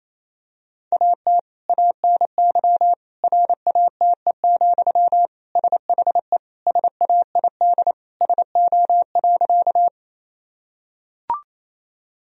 Morse code, heard as a keyed tone, about 28 wpm, 700 Hz. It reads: AT ANY RATE, SHE SAID SO.
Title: Sentences from Top 1000 Words - Review Part 4 - 28wpm